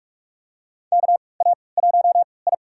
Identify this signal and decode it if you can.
Morse code, heard as a keyed tone, about 45 wpm, spaced as for 25 wpm, 700 Hz. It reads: KA1I
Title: USA Call Signs - 3x Character Spacing - 45wpm